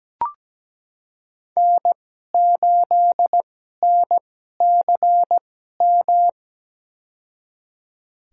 Morse code, heard as a keyed tone, about 17 wpm, spaced as for 12 wpm, 700 Hz. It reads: N8NCM
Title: USA Call Signs - 2x Character Spacing - 17wpm